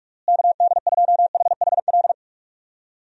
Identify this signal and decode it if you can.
Morse code, heard as a keyed tone, about 41 wpm, 700 Hz. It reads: KD2HHL
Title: USA Call Signs - 6 Characters - Repeated 3x - 45wpm